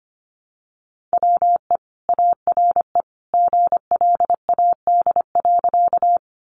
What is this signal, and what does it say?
Morse code, heard as a keyed tone, about 25 wpm, 700 Hz. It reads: WE ARE GLAD.
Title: Sentences from Top 800 Words 25wpm